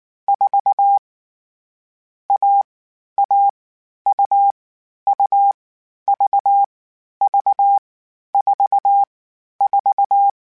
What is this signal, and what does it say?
Morse code, heard as a keyed tone, about 19 wpm, 800 Hz. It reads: 4 A A U U V V 4 4
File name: CW Code Groups-mp3s / Sets mp3s / AUV AUV4 sp @18